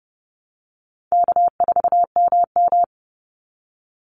Morse code, heard as a keyed tone, about 31 wpm, 700 Hz. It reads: K4MM
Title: Easy Call Signs - 30wpm